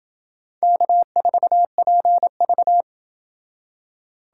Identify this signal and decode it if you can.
Morse code, heard as a keyed tone, about 28 wpm, 700 Hz. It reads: K4PV